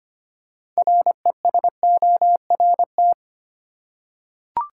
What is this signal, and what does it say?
Morse code, heard as a keyed tone, about 25 wpm, 700 Hz. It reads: RESORT
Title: Six Letter Words - 25wpm